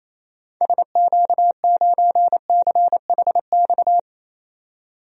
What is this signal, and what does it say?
Morse code, heard as a keyed tone, about 28 wpm, 700 Hz. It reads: SQ9CHX
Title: DXCC-100 Call Signs - 6 Characters - 28wpm